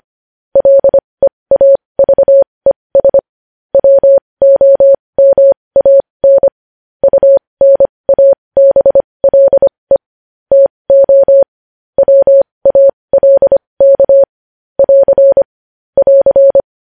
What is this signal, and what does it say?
Morse code, heard as a keyed tone, about 25 wpm, spaced as for 20 wpm, 550 Hz. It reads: LEAVES WOMAN UNABLE TO WALK <AR> <AR>
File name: podcast / News headlines-1x-25x20